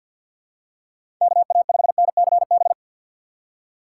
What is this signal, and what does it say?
Morse code, heard as a keyed tone, about 50 wpm, 700 Hz. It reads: KA5NCB
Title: USA Call Signs - 6 Characters - 4x Character Spacing - 50wpm